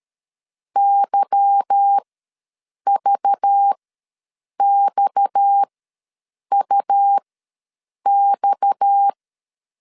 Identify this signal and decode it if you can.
Morse code, heard as a keyed tone, about 13 wpm, spaced as for 7 wpm, 800 Hz. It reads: YVXUX
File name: LEZIONI / 8 YJ45 / 8 Y ws 12wpm 3409